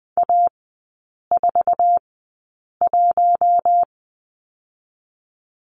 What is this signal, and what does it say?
Morse code, heard as a keyed tone, about 20 wpm, spaced as for 8 wpm, 700 Hz. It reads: A41
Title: 047 Lesson 9e - Number 4 - Three Characters 20wpm